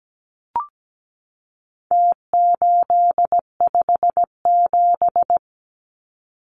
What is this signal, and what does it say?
Morse code, heard as a keyed tone, about 17 wpm, 700 Hz. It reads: T857
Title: Numbers - Contest Style - 4-Digit 17wpm